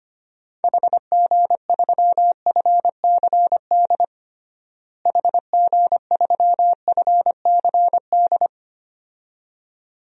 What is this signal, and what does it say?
Morse code, heard as a keyed tone, about 25 wpm, 700 Hz. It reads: HG3FCD HG3FCD
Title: DXCC-100 Call Signs - 6 Characters - Repeated 2x - 25wpm